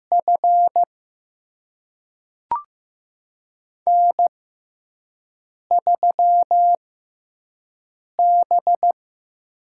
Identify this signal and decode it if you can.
Morse code, heard as a keyed tone, about 15 wpm, spaced as for 5 wpm, 700 Hz. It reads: F N3B
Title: USA Call Signs - 5 Characters - 6x Character Spacing - 15wpm